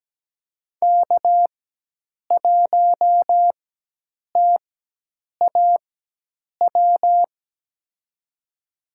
Morse code, heard as a keyed tone, about 17 wpm, spaced as for 8 wpm, 700 Hz. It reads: K1TAW
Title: USA Call Signs - 5 Characters - 4x Character Spacing - 17wpm